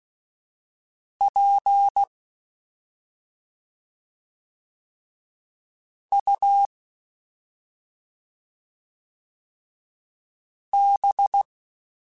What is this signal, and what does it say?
Morse code, heard as a keyed tone, about 16 wpm, spaced as for 2 wpm, 800 Hz. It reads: PUB
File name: Special INGs-mp3s / Special INGs-mp3s / Slow ing @10